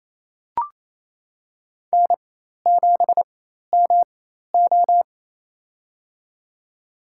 Morse code, heard as a keyed tone, about 28 wpm, spaced as for 13 wpm, 700 Hz. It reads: N7MO
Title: USA Call Signs - 4 Characters - 4x Character Spacing - 28wpm